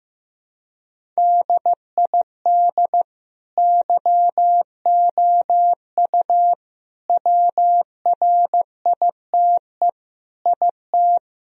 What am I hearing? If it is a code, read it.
Morse code, heard as a keyed tone, about 15 wpm, 700 Hz. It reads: DID YOU WRITE IT
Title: Sentences from Top 1000 Words - Review Part 1 - 15wpm